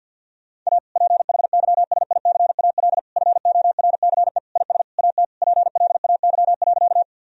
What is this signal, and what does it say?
Morse code, heard as a keyed tone, about 50 wpm, 700 Hz. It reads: A PHYSICAL FORCE IS AT PLAY.
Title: Sentences from Top 1000 Words - Review Part 4 - 50wpm